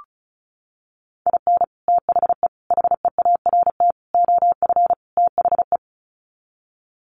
Morse code, heard as a keyed tone, about 34 wpm, 700 Hz. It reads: IN THE HEART OF THE
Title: Sets of 5 Words 35wpm